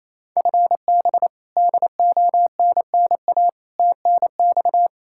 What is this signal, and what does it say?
Morse code, heard as a keyed tone, about 28 wpm, 700 Hz. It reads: FB DONNA TNX
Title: Practice QSO - Exchange 3 - More Info - 28wpm